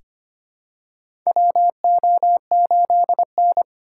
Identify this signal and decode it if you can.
Morse code, heard as a keyed tone, about 25 wpm, 700 Hz. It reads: WO8N
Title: USA Call Signs - 4 Characters - 2x Character Spacing - 25wpm